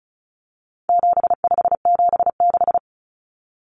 Morse code, heard as a keyed tone, about 34 wpm, 700 Hz. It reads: 7576
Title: Common Cut Numbers - 4-Digit 35wpm